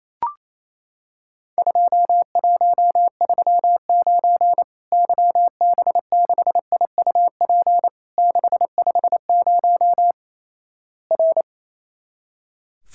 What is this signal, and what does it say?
Morse code, heard as a keyed tone, about 28 wpm, 700 Hz. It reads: 2139 YB6IUP 650 R
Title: Worked All Europe DX Contest - QTC Exchange - 28wpm